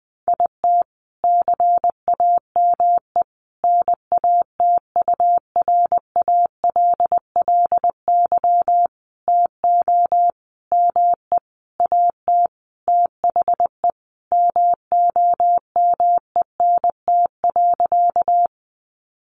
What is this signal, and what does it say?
Morse code, heard as a keyed tone, about 20 wpm, 700 Hz. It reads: IT CAME NATURALLY TO ME AT THE MOMENT.
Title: Sentences from Top 1300 Words 20wpm